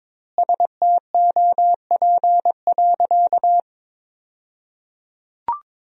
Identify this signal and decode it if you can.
Morse code, heard as a keyed tone, about 22 wpm, 700 Hz. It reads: STOP.
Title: 2x Word-Spacing - Sentences from Top 300 Words - 22wpm